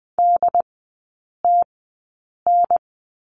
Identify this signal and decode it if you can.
Morse code, heard as a keyed tone, about 20 wpm, spaced as for 8 wpm, 700 Hz. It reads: DTN
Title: 067 Lesson 12e - Letter d - Three Characters 20wpm